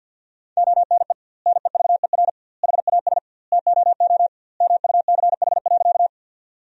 Morse code, heard as a keyed tone, about 50 wpm, 700 Hz. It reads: ONE NEVER HAS TOO MUCH.
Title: 2x Word-Spacing - Sentences from Top 300 Words - 50wpm